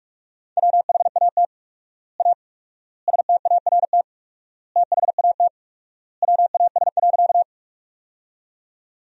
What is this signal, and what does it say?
Morse code, heard as a keyed tone, about 46 wpm, 700 Hz. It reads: WHAT A START THAT WAS.
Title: 4x Word-Spacing - Sentences from Top 300 Words - 45wpm